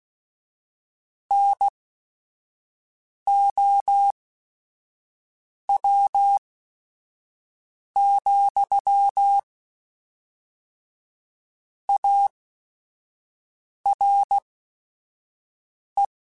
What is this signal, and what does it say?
Morse code, heard as a keyed tone, about 16 wpm, spaced as for 5 wpm, 800 Hz. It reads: NOW, ARE